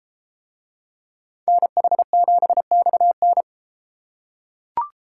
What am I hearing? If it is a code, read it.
Morse code, heard as a keyed tone, about 32 wpm, 700 Hz. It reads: NH7XN